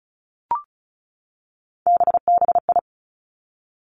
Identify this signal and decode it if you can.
Morse code, heard as a keyed tone, about 36 wpm, 700 Hz. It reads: BBI
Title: Three Challenging Letter-Number Combinations 35wpm